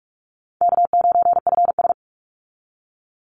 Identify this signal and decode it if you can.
Morse code, heard as a keyed tone, about 44 wpm, 700 Hz. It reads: K9FS
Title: Easy Call Signs Repeated 2x - 45wpm